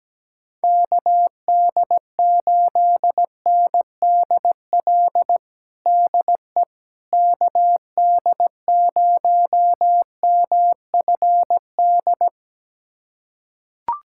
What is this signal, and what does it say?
Morse code, heard as a keyed tone, about 17 wpm, 700 Hz. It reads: KD8NDL DE KD0MFD
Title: USA Callsign DE Callsign 17wpm